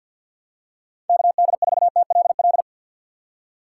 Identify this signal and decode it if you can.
Morse code, heard as a keyed tone, about 49 wpm, 700 Hz. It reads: KD4TLL
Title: USA Call Signs - 6 Characters - 6x Character Spacing - 50wpm